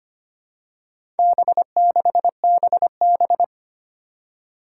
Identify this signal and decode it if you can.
Morse code, heard as a keyed tone, about 25 wpm, 700 Hz. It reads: B6BB
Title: Four Challenging Letter-Number Combinations 25wpm